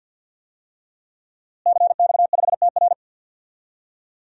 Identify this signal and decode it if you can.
Morse code, heard as a keyed tone, about 47 wpm, 700 Hz. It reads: CX5TR
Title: DXCC-100 Call Signs - 5 Characters - Repeated 3x - 50wpm